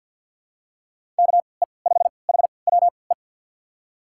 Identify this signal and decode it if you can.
Morse code, heard as a keyed tone, about 48 wpm, spaced as for 28 wpm, 700 Hz. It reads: KE5HWE
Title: USA Call Signs - 6 Characters - 3x Character Spacing - 50wpm